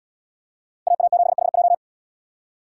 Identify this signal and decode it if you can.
Morse code, heard as a keyed tone, about 77 wpm, 700 Hz. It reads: AA7HJ